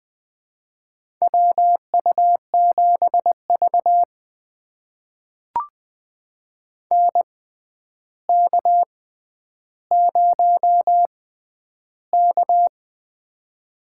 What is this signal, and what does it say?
Morse code, heard as a keyed tone, about 20 wpm, spaced as for 10 wpm, 700 Hz. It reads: WU7V NK0K
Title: USA Call Signs - 6x Character Spacing - 20wpm